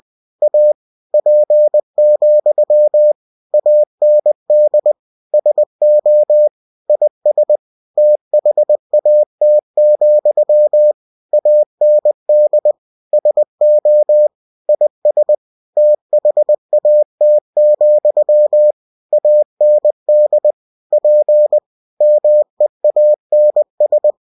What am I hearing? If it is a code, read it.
Morse code, heard as a keyed tone, about 20 wpm, 600 Hz. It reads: A P, AND SO IS THAT, AND SO IS THAT, AND P MEANS